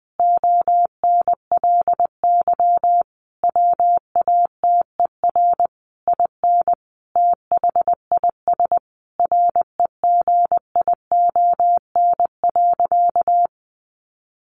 Morse code, heard as a keyed tone, about 20 wpm, 700 Hz. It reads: ONLY WATER IN THIS REGION.
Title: Sentences from Top 1500 Words 20wpm